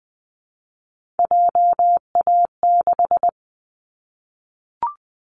Morse code, heard as a keyed tone, about 20 wpm, 700 Hz. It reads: JA6
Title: Callsign Prefixes for Top 25 DXCC - 20wpm